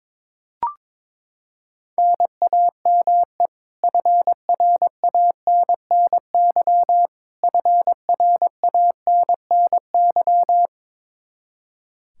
Morse code, heard as a keyed tone, about 21 wpm, 700 Hz. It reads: NAME FRANNY FRANNY
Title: Practice QSO - Exchange 1 - Beginning - 22wpm